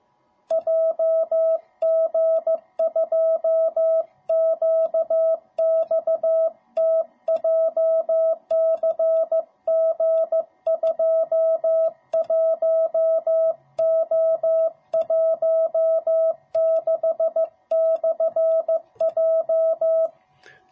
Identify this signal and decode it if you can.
Morse code, heard as a keyed tone, about 15 wpm, 660 Hz. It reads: JG2QXT JCG21O16/J